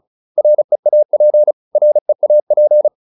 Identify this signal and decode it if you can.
Morse code, heard as a keyed tone, about 34 wpm, 600 Hz. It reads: REAP REAP